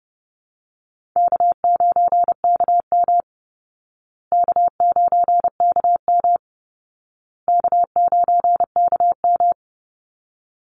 Morse code, heard as a keyed tone, about 30 wpm, 700 Hz. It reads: K9KM K9KM K9KM
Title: Easy Call Signs Repeated 3x - 30wpm